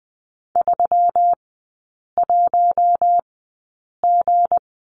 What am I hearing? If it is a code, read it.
Morse code, heard as a keyed tone, about 20 wpm, spaced as for 8 wpm, 700 Hz. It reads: 31G
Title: 191 Lesson 31e - Letter v - Three Characters 20wpm